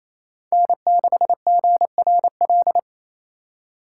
Morse code, heard as a keyed tone, about 27 wpm, 700 Hz. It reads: N6GRL
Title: USA Call Signs - 5 Characters - 4x Character Spacing - 28wpm